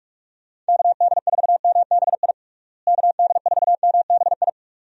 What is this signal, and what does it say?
Morse code, heard as a keyed tone, about 45 wpm, 700 Hz. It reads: KD4MBI KD4MBI
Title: USA Call Signs - 6 Characters - Repeated 2x - 45wpm